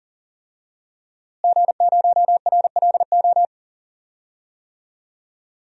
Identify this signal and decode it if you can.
Morse code, heard as a keyed tone, about 37 wpm, 700 Hz. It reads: G0RLO
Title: DXCC-100 Call Signs - 5 Characters - Repeated 3x - 40wpm